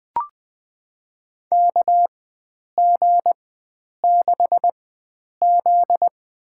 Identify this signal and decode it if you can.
Morse code, heard as a keyed tone, about 20 wpm, spaced as for 9 wpm, 700 Hz. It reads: KG6Z
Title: USA Call Signs - 4x Character Spacing - 20wpm